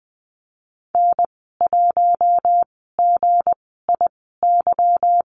Morse code, heard as a keyed tone, about 20 wpm, spaced as for 15 wpm, 700 Hz. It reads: N1GIY
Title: 179 Lesson 29g - Slash - Call Signs 20wpm